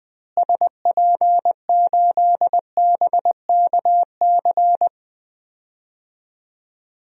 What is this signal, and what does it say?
Morse code, heard as a keyed tone, about 20 wpm, 700 Hz. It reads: SP8BKC
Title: DXCC-100 Call Signs - 6 Characters - Repeated 2x - 20wpm